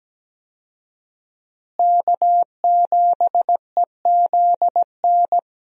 Morse code, heard as a keyed tone, about 17 wpm, 700 Hz. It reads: K7EZN
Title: USA Call Signs - 5 Characters - Repeated 2x - 17wpm